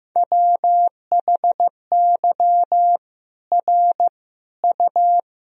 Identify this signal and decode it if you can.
Morse code, heard as a keyed tone, about 15 wpm, 700 Hz. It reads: WHY R U
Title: Sentences from Top 200 Words and Abbreviations 15wpm